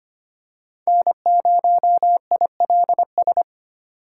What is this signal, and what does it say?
Morse code, heard as a keyed tone, about 25 wpm, 700 Hz. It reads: N0ILS
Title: USA Call Signs - 5 Characters - 4x Character Spacing - 25wpm